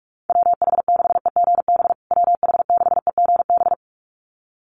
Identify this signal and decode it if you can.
Morse code, heard as a keyed tone, about 44 wpm, 700 Hz. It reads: WH6EGB WH6EGB
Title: USA Call Signs Repeated 2x 45wpm